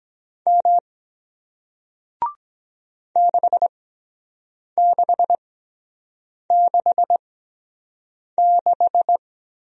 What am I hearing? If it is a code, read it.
Morse code, heard as a keyed tone, about 24 wpm, spaced as for 7 wpm, 700 Hz. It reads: M 6666